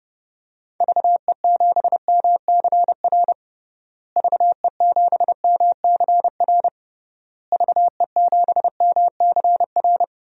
Morse code, heard as a keyed tone, about 32 wpm, 700 Hz. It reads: VE7MCR VE7MCR VE7MCR
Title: DXCC-100 Call Signs - 6 Characters - Repeated 3x - 30wpm